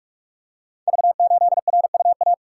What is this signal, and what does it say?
Morse code, heard as a keyed tone, about 44 wpm, 700 Hz. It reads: V8RUA
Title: Code Groups - 5 Letter:Numbers - 45wpm